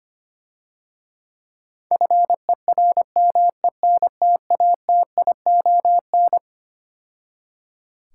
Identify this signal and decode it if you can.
Morse code, heard as a keyed tone, about 25 wpm, 690 Hz. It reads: FERMENTATION
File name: practice / 25wpm / Difficult Words - Speed-Racing with Hints - 25wpm